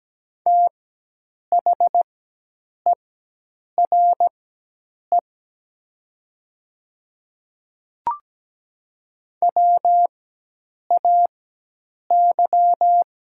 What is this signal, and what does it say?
Morse code, heard as a keyed tone, about 17 wpm, spaced as for 8 wpm, 700 Hz. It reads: THERE WAY
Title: 4x Character Spacing - Top 100 Words - 17wpm